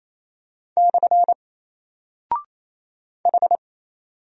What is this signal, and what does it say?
Morse code, heard as a keyed tone, about 27 wpm, 700 Hz. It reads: / H
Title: Single Character (Letter:Number:Punctuation:Prosign) Alt 28wpm